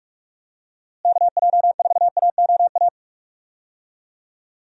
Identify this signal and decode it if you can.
Morse code, heard as a keyed tone, about 46 wpm, 700 Hz. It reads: KJ4AOA
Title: USA Call Signs - 6 Characters - Repeated 2x - 45wpm